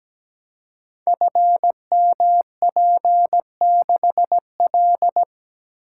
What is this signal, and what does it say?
Morse code, heard as a keyed tone, about 17 wpm, 700 Hz. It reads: FMP6L
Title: Code Groups - 5 Letter:Numbers - 17wpm